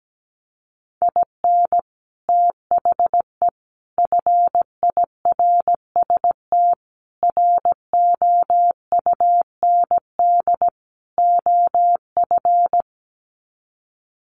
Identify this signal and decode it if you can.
Morse code, heard as a keyed tone, about 17 wpm, 700 Hz. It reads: IN THE FIRST ROUND OF
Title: Sets of 5 Words Encore 17wpm